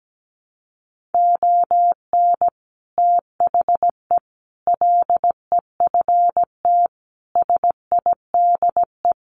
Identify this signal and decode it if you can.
Morse code, heard as a keyed tone, about 17 wpm, 700 Hz. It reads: ON THE LEFT SIDE
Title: Sets of 5 Words Encore2 17wpm